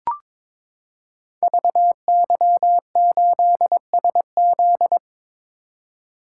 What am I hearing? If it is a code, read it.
Morse code, heard as a keyed tone, about 22 wpm, 700 Hz. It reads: VY8SZ
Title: Code Groups - 5 Letter:Numbers - 22wpm